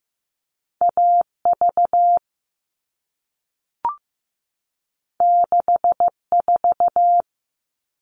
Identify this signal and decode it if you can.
Morse code, heard as a keyed tone, about 15 wpm, 700 Hz. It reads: AV 64
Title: All Cut Numbers - 2-Digit 15wpm